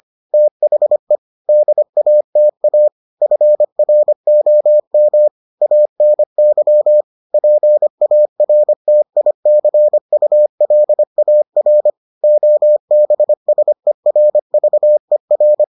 Morse code, heard as a keyed tone, about 25 wpm, 600 Hz. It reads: THE DATA FROM ANY PARTICULAR OBSERVER